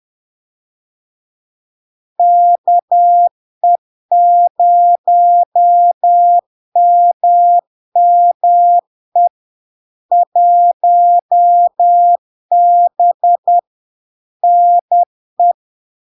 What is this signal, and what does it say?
Morse code, heard as a keyed tone, about 10 wpm, 700 Hz. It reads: KE0MME 1B NE
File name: LARK-FD-2022-Log-in-CW-10wpm-700-Hz